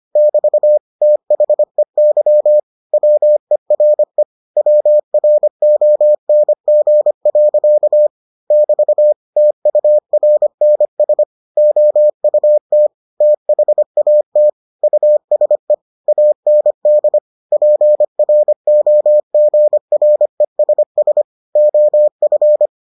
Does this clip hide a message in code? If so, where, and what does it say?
Yes: Morse code from 0.2 to 22.7 s, heard as a keyed tone, about 25 wpm, 600 Hz. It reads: <BT> THEY WERE WRONG. <BT> TURNS OUT THAT USE AND PROGRESS OF